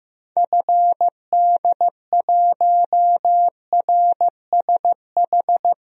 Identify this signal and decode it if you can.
Morse code, heard as a keyed tone, about 15 wpm, 700 Hz. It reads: FD1RSH
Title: DXCC-100 Call Signs - 6 Characters - 15wpm